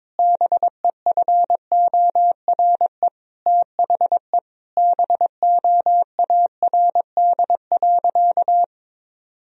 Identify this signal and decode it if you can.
Morse code, heard as a keyed tone, about 22 wpm, 700 Hz. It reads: BEFORE THE BOARD.
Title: Sentences from Top 1000 Words - Review Part 4 - 22wpm